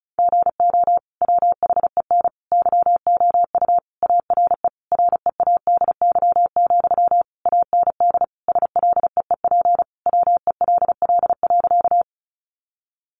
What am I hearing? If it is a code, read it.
Morse code, heard as a keyed tone, about 34 wpm, 700 Hz. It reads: GO WHEN YOU ARE READY, AND SLEEP WELL.
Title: Sentences from Top 500 Words 35wpm